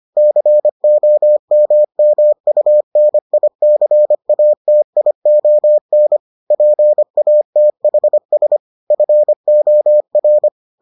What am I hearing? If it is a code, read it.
Morse code, heard as a keyed tone, about 25 wpm, 600 Hz. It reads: COMMUNICATION PATHS FOR